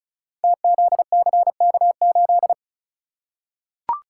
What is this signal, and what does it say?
Morse code, heard as a keyed tone, about 36 wpm, 700 Hz. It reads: TZCK8
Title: Code Groups - 5 Letter:Numbers - 35wpm